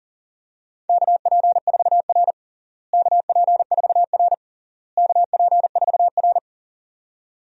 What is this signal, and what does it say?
Morse code, heard as a keyed tone, about 40 wpm, 700 Hz. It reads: KP4R KP4R KP4R
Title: DXCC-100 Call Signs - 4 Characters - Repeated 3x - 40wpm